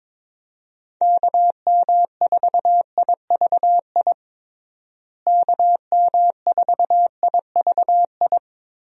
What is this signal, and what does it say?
Morse code, heard as a keyed tone, about 22 wpm, 700 Hz. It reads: KM4IVI KM4IVI
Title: USA Call Signs - 6 Characters - Repeated 2x - 22wpm